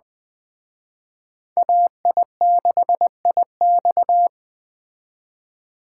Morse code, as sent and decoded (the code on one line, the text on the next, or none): .- .. -.... .. -..-
AI6IX